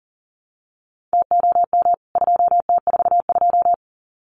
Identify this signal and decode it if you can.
Morse code, heard as a keyed tone, about 40 wpm, 700 Hz. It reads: TOM 2T42
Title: CWT Exchange 40wpm